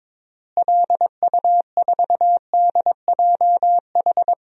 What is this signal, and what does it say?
Morse code, heard as a keyed tone, about 22 wpm, 700 Hz. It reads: LU4DJH
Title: DXCC-100 Call Signs - 6 Characters - Repeated 2x - 22wpm